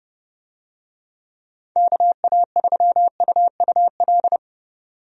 Morse code, heard as a keyed tone, about 30 wpm, 700 Hz. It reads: KA3UUL
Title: USA Call Signs - 6 Characters - 30wpm